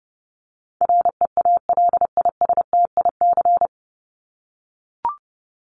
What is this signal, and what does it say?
Morse code, heard as a keyed tone, about 30 wpm, 700 Hz. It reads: REALISTIC
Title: Top 1751-2000 Words 30wpm